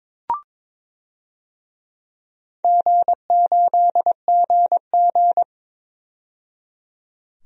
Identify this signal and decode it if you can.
Morse code, heard as a keyed tone, about 22 wpm, 700 Hz. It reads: G8GG
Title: DXCC-100 Call Signs - 4 Characters - 22wpm